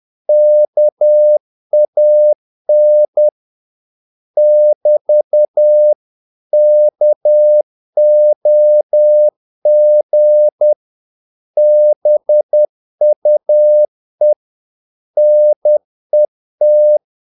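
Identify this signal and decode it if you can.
Morse code, heard as a keyed tone, about 10 wpm, 600 Hz. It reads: KAN <BT> KOG BUE NET
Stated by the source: Morse practice of short Danish words at 10 wpm, from CW med Gnister.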